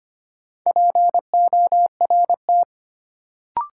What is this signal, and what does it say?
Morse code, heard as a keyed tone, about 25 wpm, 700 Hz. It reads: PORT
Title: Sentences from Top 1000 Words - Review Part 1 - New Word Drill - 25wpm